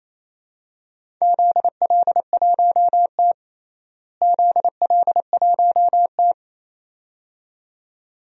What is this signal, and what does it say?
Morse code, heard as a keyed tone, about 28 wpm, 700 Hz. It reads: ZL1T ZL1T
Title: DXCC-100 Call Signs - 4 Characters - Repeated 2x - 28wpm